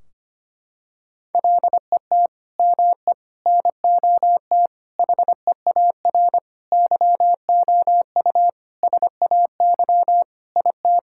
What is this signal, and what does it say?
Morse code, heard as a keyed tone, about 25 wpm, 700 Hz. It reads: LET ME NOT HEAR YOU SAY IT
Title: Sentences from Top 1600 Words 25wpm